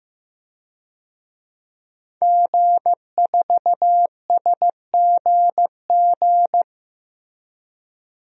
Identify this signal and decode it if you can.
Morse code, heard as a keyed tone, about 15 wpm, 700 Hz. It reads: G4SGG
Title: DXCC-100 Call Signs - 5 Characters - 15wpm